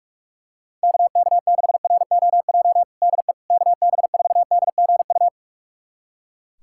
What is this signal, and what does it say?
Morse code, heard as a keyed tone, about 45 wpm, 700 Hz. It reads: KK6ROJ DE KB4DGU
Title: USA Callsign DE Callsign - 6 Characters - 45wpm